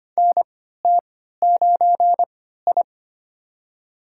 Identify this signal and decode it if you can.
Morse code, heard as a keyed tone, about 25 wpm, spaced as for 14 wpm, 700 Hz. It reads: NT9I
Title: USA Call Signs - 4 Characters - 3x Character Spacing - 25wpm